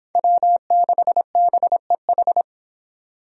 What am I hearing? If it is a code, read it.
Morse code, heard as a keyed tone, about 23 wpm, 700 Hz. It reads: W6BEH